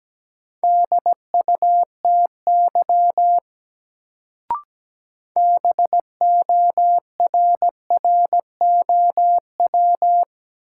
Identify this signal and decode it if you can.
Morse code, heard as a keyed tone, about 17 wpm, 700 Hz. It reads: DUTY BORROW
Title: Sentences from Top 1500 Words - New Word Drill - 17wpm